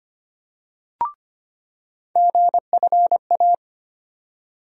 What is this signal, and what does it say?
Morse code, heard as a keyed tone, about 25 wpm, 700 Hz. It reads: GFA
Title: USA Call Sign Postfixes - 3 Letters 25wpm